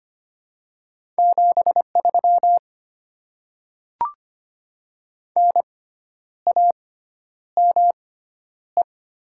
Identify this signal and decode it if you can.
Morse code, heard as a keyed tone, about 25 wpm, spaced as for 10 wpm, 700 Hz. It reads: 73 NAME
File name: Top 20 QSO Elements - 6x Character Spacing - 25wpm